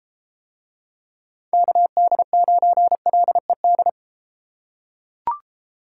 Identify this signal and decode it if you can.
Morse code, heard as a keyed tone, about 31 wpm, 700 Hz. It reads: KD9LED